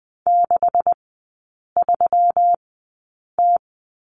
Morse code, heard as a keyed tone, about 20 wpm, spaced as for 8 wpm, 700 Hz. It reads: 63T
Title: 127 Lesson 21e - Number 6 - Three Characters 20wpm